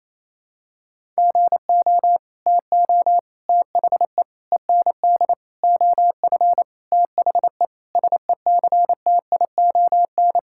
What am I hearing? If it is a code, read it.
Morse code, heard as a keyed tone, about 28 wpm, 700 Hz. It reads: GO TO THE END OF THE SECTION
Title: Sentences from Top 1000 Words - Review Part 5 - 28wpm